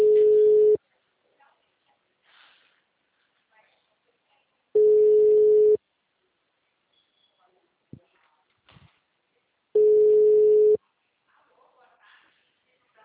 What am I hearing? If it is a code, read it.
Ringback tone (Europe, 425 Hz), with speech.
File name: out- -2006- - - .68232